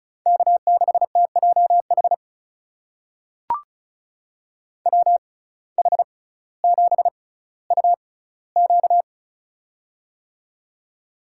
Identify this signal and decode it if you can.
Morse code, heard as a keyed tone, about 36 wpm, spaced as for 18 wpm, 700 Hz. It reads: K6TJH WH7UQ
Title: USA Call Signs - 6x Character Spacing - 35wpm